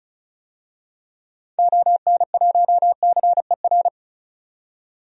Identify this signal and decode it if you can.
Morse code, heard as a keyed tone, about 36 wpm, 700 Hz. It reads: ON1CER
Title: DXCC-100 Call Signs - 6 Characters - 35wpm